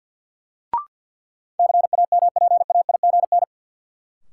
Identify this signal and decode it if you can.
Morse code, heard as a keyed tone, about 53 wpm, 700 Hz. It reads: CAMPAIGN
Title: Sentences from Top 1200 Words - New Word Drill - 50wpm